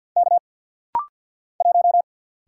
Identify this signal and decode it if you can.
Morse code, heard as a keyed tone, about 50 wpm, 690 Hz. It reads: K 1
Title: Single Letter-Number Alt - Rapid-Fire 50wpm